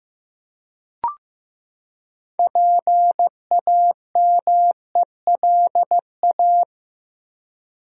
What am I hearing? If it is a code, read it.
Morse code, heard as a keyed tone, about 15 wpm, 700 Hz. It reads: PAMELA
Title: USA Common Names - 15wpm